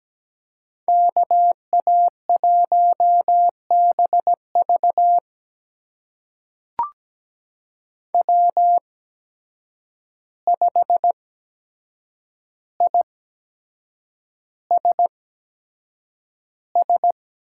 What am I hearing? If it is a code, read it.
Morse code, heard as a keyed tone, about 17 wpm, spaced as for 7 wpm, 700 Hz. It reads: KA1BV W5ISS
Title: USA Call Signs - 8x Character Spacing - 17wpm